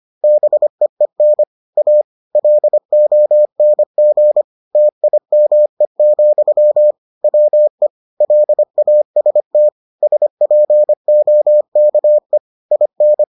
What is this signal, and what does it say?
Morse code, heard as a keyed tone, about 25 wpm, 600 Hz. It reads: BEEN A LONG TIME, WE LAST SPOKE IN